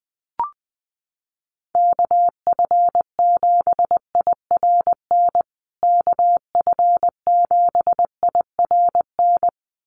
Morse code, heard as a keyed tone, about 20 wpm, 700 Hz. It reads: KF7IRN KF7IRN
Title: USA Call Signs Repeated 2x 20wpm